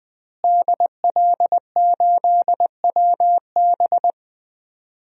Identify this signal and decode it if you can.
Morse code, heard as a keyed tone, about 21 wpm, 700 Hz. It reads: DL8WB
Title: DXCC-100 Call Signs - 5 Characters - Repeated 2x - 20wpm